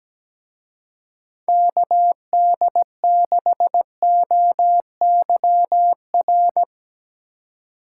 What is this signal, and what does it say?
Morse code, heard as a keyed tone, about 17 wpm, 700 Hz. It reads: KD6OYR
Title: Speed Racing - USA 2x3 Call Signs 17wpm